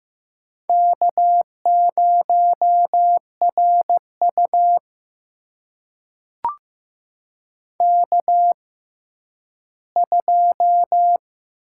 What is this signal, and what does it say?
Morse code, heard as a keyed tone, about 15 wpm, spaced as for 10 wpm, 700 Hz. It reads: K0RU K2
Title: USA Call Signs - 4 Characters - 6x Character Spacing - 15wpm